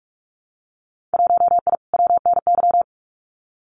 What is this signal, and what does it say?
Morse code, heard as a keyed tone, about 46 wpm, 700 Hz. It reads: 1I WNY
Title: Winter Field Day Exchanges Encore 45wpm